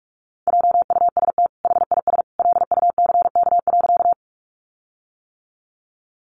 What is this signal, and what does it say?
Morse code, heard as a keyed tone, about 44 wpm, 700 Hz. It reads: JUST HIS LUCK.
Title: Sentences from Top 1500 Words 45wpm